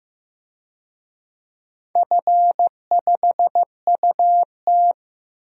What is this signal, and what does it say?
Morse code, heard as a keyed tone, about 15 wpm, 700 Hz. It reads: F5UT
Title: DXCC-100 Call Signs - 4 Characters - 15wpm